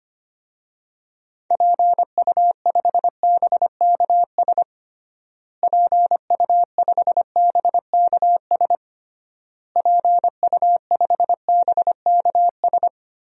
Morse code, heard as a keyed tone, about 25 wpm, 700 Hz. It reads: PU5BKS PU5BKS PU5BKS
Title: DXCC-100 Call Signs - 6 Characters - Repeated 3x - 25wpm